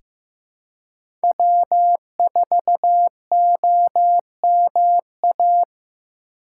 Morse code, heard as a keyed tone, about 15 wpm, 700 Hz. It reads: W4OMA
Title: USA Call Signs - 5 Characters - 2x Character Spacing - 15wpm